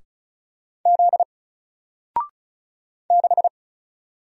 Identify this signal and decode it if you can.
Morse code, heard as a keyed tone, about 36 wpm, 700 Hz. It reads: Z 6
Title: Single Letter-Number 35wpm